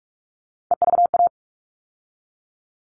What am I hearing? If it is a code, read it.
Morse code, heard as a keyed tone, about 44 wpm, 700 Hz. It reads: EVA
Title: All Cut Numbers - 3-Digit 45wpm